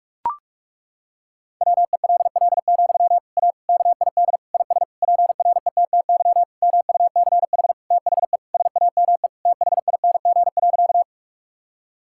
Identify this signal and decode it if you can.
Morse code, heard as a keyed tone, about 41 wpm, 700 Hz. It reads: WELL, A KID IS PRETTY MUCH THE SAME THING.
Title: Sentences from Top 1600 Words 45wpm